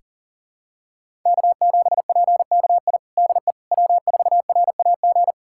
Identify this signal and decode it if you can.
Morse code, heard as a keyed tone, about 40 wpm, 700 Hz. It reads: K7PKI DE W4RAG
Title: USA Callsign DE Callsign - 5 Characters - 40wpm